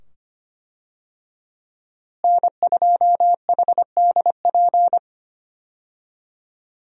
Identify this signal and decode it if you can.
Morse code, heard as a keyed tone, about 25 wpm, 700 Hz. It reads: N2HDP
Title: USA Call Signs - 5 Characters - Repeated 2x - 25wpm